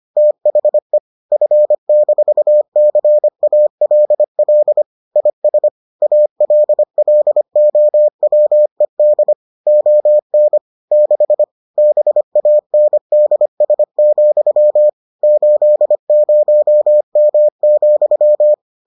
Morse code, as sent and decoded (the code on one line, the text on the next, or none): - .... / ..-. -....- -.-. .- .-.. .-.. / .. ... / .- .-.. .-.. --- .-- . -.. / --- -. / -.... / -... .- -. -.. ... --..-- / ---.. ----- -- --..--
THE F-CALL IS ALLOWED ON 6 BANDS, 80M,